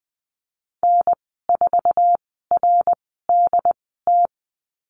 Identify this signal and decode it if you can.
Morse code, heard as a keyed tone, about 20 wpm, spaced as for 15 wpm, 700 Hz. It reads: N4RDT
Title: 076 Lesson 13g - Letter L - Call Signs 20wpm